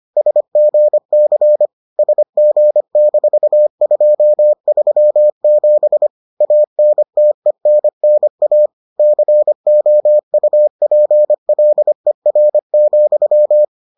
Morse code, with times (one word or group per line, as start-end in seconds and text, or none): 0.2-1.7 SGC
2.0-6.1 SG-237
6.4-8.7 ANTENNA
9.0-13.6 COUPLER,